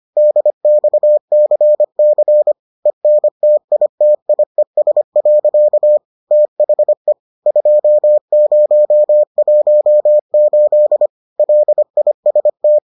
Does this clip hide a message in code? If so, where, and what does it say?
Yes: Morse code from 0.2 to 12.8 s, heard as a keyed tone, about 25 wpm, 600 Hz. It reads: DXCC ENTITIES. THE 2018 LIST